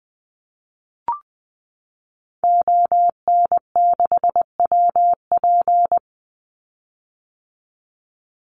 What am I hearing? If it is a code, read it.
Morse code, heard as a keyed tone, about 20 wpm, 700 Hz. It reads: ON6WP
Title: Call Signs - 20wpm